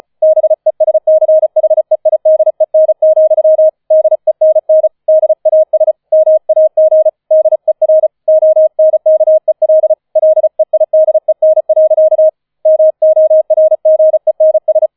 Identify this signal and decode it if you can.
Morse code, heard as a keyed tone, about 36 wpm, 630 Hz. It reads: BESCHEIDEN, DENN DAS MAG DER ONKEL LEIDEN. MORGENS
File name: 005 Fünfter Streich 35wpm